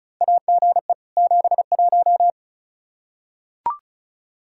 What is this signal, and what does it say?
Morse code, heard as a keyed tone, about 36 wpm, 700 Hz. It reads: AGE 71
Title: Practice QSO - Exchange 3 - More Info - 35wpm